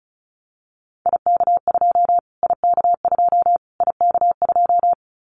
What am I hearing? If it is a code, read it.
Morse code, heard as a keyed tone, about 34 wpm, 700 Hz. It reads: IK2 IK2 IK2
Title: Callsign Prefixes for Top 25 DXCC - Repeated 3x - 35wpm